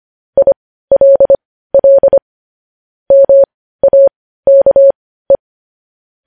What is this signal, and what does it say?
Morse code, heard as a keyed tone, about 25 wpm, spaced as for 15 wpm, 550 Hz. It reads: ILL MAKE
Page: news-headlines-in-morse-code-at-15-wpm